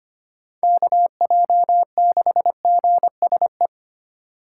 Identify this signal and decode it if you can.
Morse code, heard as a keyed tone, about 31 wpm, 700 Hz. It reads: KJ6GSE
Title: Speed Racing - USA 2x3 Call Signs 25wpm